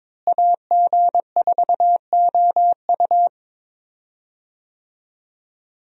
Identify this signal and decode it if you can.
Morse code, heard as a keyed tone, about 22 wpm, 700 Hz. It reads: AG4OU
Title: USA Call Signs - 5 Characters - Repeated 3x - 22wpm